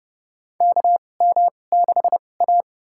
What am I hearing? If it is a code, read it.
Morse code, heard as a keyed tone, about 30 wpm, spaced as for 22 wpm, 700 Hz. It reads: KM6A
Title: USA Call Signs - 2x Character Spacing - 30wpm